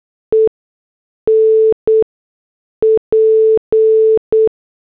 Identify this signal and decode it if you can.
Morse code, heard as a keyed tone, about 8 wpm, spaced as for 6 wpm, 440 Hz. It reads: EN P